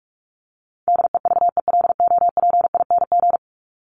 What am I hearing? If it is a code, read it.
Morse code, heard as a keyed tone, about 44 wpm, 700 Hz. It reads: DEVELOPING
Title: ING Suffix - Top 100 Words with Base Words 45wpm